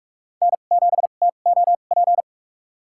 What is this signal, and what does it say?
Morse code, heard as a keyed tone, about 46 wpm, spaced as for 32 wpm, 700 Hz. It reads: N7TOP
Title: USA Call Signs - 2x Character Spacing - 45wpm